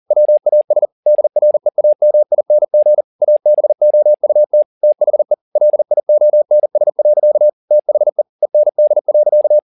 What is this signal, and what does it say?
Morse code, heard as a keyed tone, about 40 wpm, 600 Hz. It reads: WAS DREAMING ABOUT THE LIONS. THE END.
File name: the old man and the sea 2 40wpm